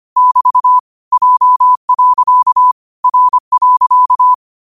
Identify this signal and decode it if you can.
Morse code, heard as a keyed tone, about 25 wpm, 1000 Hz. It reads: <BT> J. R.
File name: qotdfiles / 25wpm